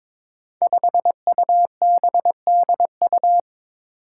Morse code, heard as a keyed tone, about 22 wpm, 700 Hz. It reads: 5UBDU
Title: Five Challenging Letter-Number Combinations 22wpm